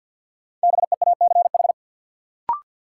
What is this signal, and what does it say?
Morse code, heard as a keyed tone, about 50 wpm, 700 Hz. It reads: BEACH
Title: Sentences from Top 1500 Words - New Word Drill Alt - 50wpm